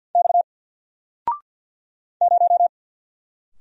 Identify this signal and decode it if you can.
Morse code, heard as a keyed tone, about 50 wpm, 700 Hz. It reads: X 0
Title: Single Letter-Number 50wpm